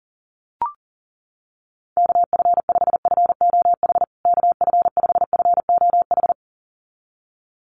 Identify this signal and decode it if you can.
Morse code, heard as a keyed tone, about 37 wpm, 700 Hz. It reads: KF5FOH KF5FOH
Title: USA Call Signs Repeated 2x 40wpm